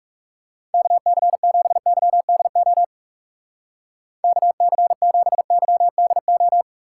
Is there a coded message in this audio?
Yes: Morse code from 0.7 to 6.6 s, heard as a keyed tone, about 38 wpm, 700 Hz. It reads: KC7YDO KC7YDO